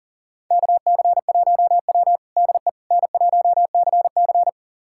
Morse code, heard as a keyed tone, about 37 wpm, 700 Hz. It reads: KC1W DE N1CC
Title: USA Callsign DE Callsign - 4 Characters - 40wpm